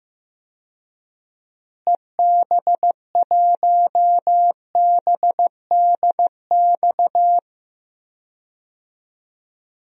Morse code, heard as a keyed tone, about 15 wpm, 700 Hz. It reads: EB1BDX
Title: DXCC-100 Call Signs - 6 Characters - Repeated 2x - 15wpm